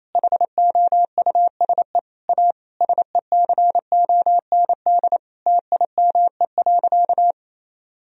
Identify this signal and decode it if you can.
Morse code, heard as a keyed tone, about 28 wpm, 700 Hz. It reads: HOUSE A SECOND TIME.
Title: Sentences from Top 1000 Words - Review Part 1 - 28wpm